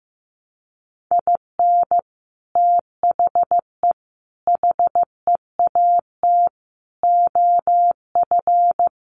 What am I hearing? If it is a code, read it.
Morse code, heard as a keyed tone, about 15 wpm, 700 Hz. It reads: IN THE HEAT OF
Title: Sets of 5 Words Encore2 15wpm